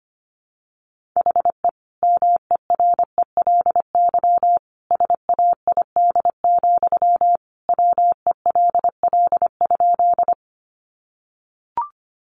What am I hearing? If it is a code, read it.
Morse code, heard as a keyed tone, about 25 wpm, 700 Hz. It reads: HE MERELY SAID, WELL?
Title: Sentences from Top 1400 Words 25wpm